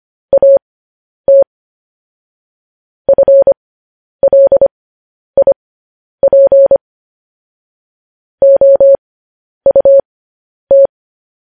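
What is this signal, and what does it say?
Morse code, heard as a keyed tone, about 25 wpm, spaced as for 10 wpm, 550 Hz. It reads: AT FLIP OUT
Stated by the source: Morse code news headlines sent as a reading transcription, preceded by vvv.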